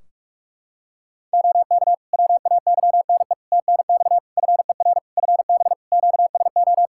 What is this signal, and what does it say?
Morse code, heard as a keyed tone, about 46 wpm, 700 Hz. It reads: OK WAYNE TNX FER FB QSO
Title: Practice QSO - Exchange 4 - Ending - 45wpm